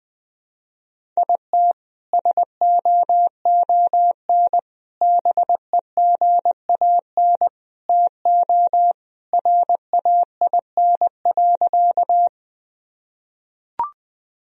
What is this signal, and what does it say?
Morse code, heard as a keyed tone, about 19 wpm, 700 Hz. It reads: IT SOON BEGAN TO RAIN.
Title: Sentences from Top 1000 Words - Review Part 1 - 20wpm